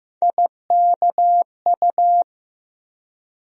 Morse code, heard as a keyed tone, about 15 wpm, 700 Hz. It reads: IKU
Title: USA Call Sign Postfixes - 3 Letters 15wpm